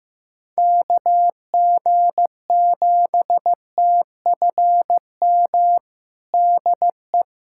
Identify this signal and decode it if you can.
Morse code, heard as a keyed tone, about 15 wpm, 700 Hz. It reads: KG7TFM DE
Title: USA Callsign DE Callsign - 6 Characters - 15wpm